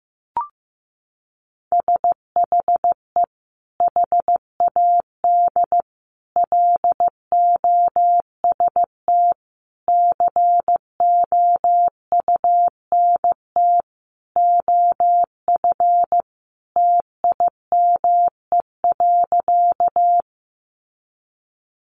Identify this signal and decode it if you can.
Morse code, heard as a keyed tone, about 15 wpm, 700 Hz. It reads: SHE HAD LOST COUNT OF TIME.
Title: Sentences from Top 1500 Words 15wpm